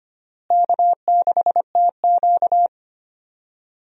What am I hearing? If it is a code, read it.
Morse code, heard as a keyed tone, about 25 wpm, 700 Hz. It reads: K6TQ
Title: USA Call Signs - 4 Characters - Repeated 3x - 25wpm